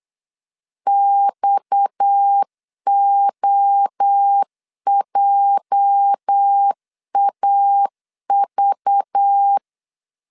Morse code, heard as a keyed tone, about 8 wpm, 800 Hz. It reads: XOJAV
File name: LEZIONI / 10 SIGNS / 10A SLASH 8wpm 3452